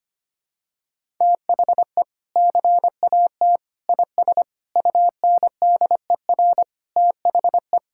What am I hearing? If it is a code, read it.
Morse code, heard as a keyed tone, about 25 wpm, 700 Hz. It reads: THE CAT IS UNDER THE
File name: practice / 25wpm / Sentences with 2-Word Phrases - 25wpm